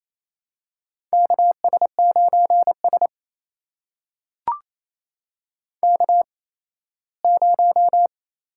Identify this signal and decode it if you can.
Morse code, heard as a keyed tone, about 28 wpm, spaced as for 17 wpm, 700 Hz. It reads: KS9S K0
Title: USA Call Signs - 4 Characters - 8x Character Spacing - 28wpm